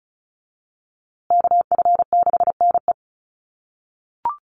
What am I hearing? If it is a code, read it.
Morse code, heard as a keyed tone, about 34 wpm, 700 Hz. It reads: KF6NE
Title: USA Call Signs Repeated 3x 35wpm